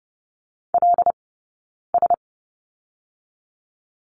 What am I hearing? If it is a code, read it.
Morse code, heard as a keyed tone, about 30 wpm, 700 Hz. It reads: L S
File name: Lesson 18d - Letter m - Two Characters 30wpm